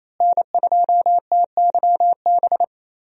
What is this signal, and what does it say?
Morse code, heard as a keyed tone, about 28 wpm, 700 Hz. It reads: N2TYB
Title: USA Call Signs - 5 Characters - Repeated 3x - 28wpm